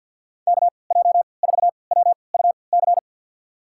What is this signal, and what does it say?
Morse code, heard as a keyed tone, about 52 wpm, spaced as for 29 wpm, 700 Hz. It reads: KJ4WUC
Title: USA Call Signs - 6 Characters - 3x Character Spacing - 50wpm